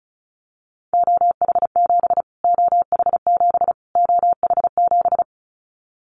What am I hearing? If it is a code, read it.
Morse code, heard as a keyed tone, about 34 wpm, 700 Hz. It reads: OH7 OH7 OH7
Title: Callsign Prefixes for Top 25 DXCC - Repeated 3x - 35wpm